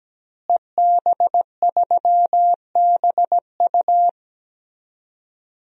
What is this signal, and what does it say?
Morse code, heard as a keyed tone, about 17 wpm, 700 Hz. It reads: EB3BU
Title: DXCC-100 Call Signs - 5 Characters - Repeated 2x - 17wpm